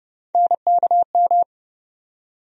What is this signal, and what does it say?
Morse code, heard as a keyed tone, about 32 wpm, 700 Hz. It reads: NKM
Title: USA Call Sign Postfixes - 3 Letters 30wpm